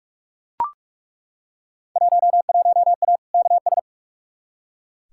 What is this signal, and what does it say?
Morse code, heard as a keyed tone, about 42 wpm, 700 Hz. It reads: 11A KS
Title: Field Day Exchanges Encore - 45wpm